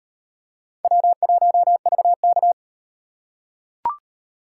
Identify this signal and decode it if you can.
Morse code, heard as a keyed tone, about 37 wpm, 700 Hz. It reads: W1VK